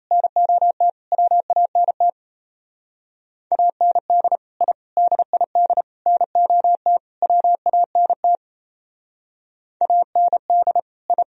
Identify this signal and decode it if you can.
Morse code, heard as a keyed tone, about 35 wpm, 700 Hz. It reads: NOT WANT AND I DID NOT WANT AND I